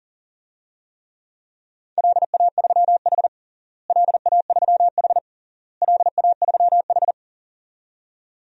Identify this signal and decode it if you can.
Morse code, heard as a keyed tone, about 40 wpm, 700 Hz. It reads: LA3H LA3H LA3H
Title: DXCC-100 Call Signs - 4 Characters - Repeated 3x - 40wpm